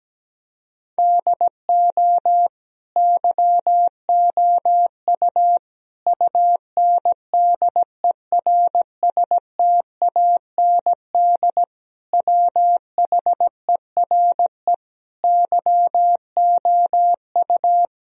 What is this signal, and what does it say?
Morse code, heard as a keyed tone, about 17 wpm, 700 Hz. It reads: DO YOU UNDERSTAND WHERE YOU
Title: Sentences from Top 1000 Words - Review Part 4 - 17wpm